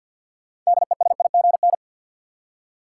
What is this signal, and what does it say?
Morse code, heard as a keyed tone, about 52 wpm, 700 Hz. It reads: DESIGN
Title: Six Letter Words - 50wpm